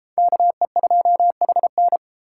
Morse code, heard as a keyed tone, about 31 wpm, 700 Hz. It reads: KE2HN